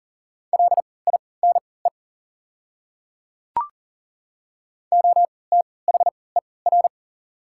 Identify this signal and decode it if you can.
Morse code, heard as a keyed tone, about 40 wpm, spaced as for 23 wpm, 690 Hz. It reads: LINE OTHER
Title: 3x Character Spacing - Top 300 Words - 40wpm